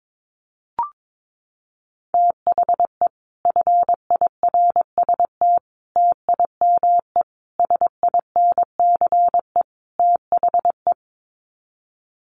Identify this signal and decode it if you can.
Morse code, heard as a keyed tone, about 22 wpm, 700 Hz. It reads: THE FIRST TIME SINCE THE
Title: Sets of 5 Words Encore 22wpm